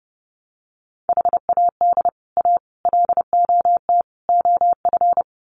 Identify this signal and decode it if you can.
Morse code, heard as a keyed tone, about 30 wpm, 700 Hz. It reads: HAD A LOT OF
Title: Sets of 4 Words 30wpm